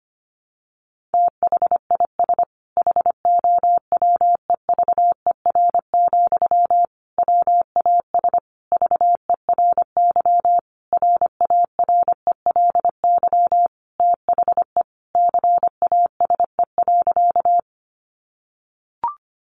Morse code, heard as a keyed tone, about 25 wpm, 700 Hz. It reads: THIS HOWEVER, WAS VERY RARELY THE CASE.
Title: Sentences from Top 1400 Words 25wpm